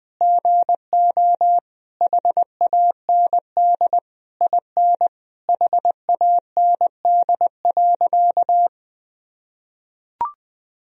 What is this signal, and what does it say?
Morse code, heard as a keyed tone, about 20 wpm, 700 Hz. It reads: GO HAND IN HAND.
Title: Sentences from Top 1000 Words - Review Part 3 - 20wpm